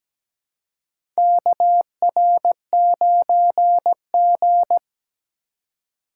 Morse code, heard as a keyed tone, about 17 wpm, 700 Hz. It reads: KR9G
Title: USA Call Signs - 4 Characters - 3x Character Spacing - 17wpm